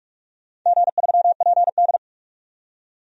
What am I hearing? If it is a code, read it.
Morse code, heard as a keyed tone, about 46 wpm, 700 Hz. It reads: G3PD
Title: DXCC-100 Call Signs - 4 Characters - 45wpm